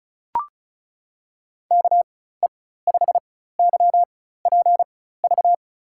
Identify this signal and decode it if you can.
Morse code, heard as a keyed tone, about 36 wpm, spaced as for 16 wpm, 700 Hz. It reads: KE5YPV
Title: USA Call Signs - 6 Characters - 4x Character Spacing - 35wpm